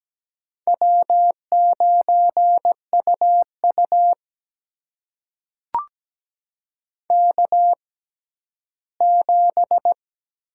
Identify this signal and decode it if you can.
Morse code, heard as a keyed tone, about 17 wpm, spaced as for 12 wpm, 700 Hz. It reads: W9UU K7
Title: USA Call Signs - 4 Characters - 6x Character Spacing - 17wpm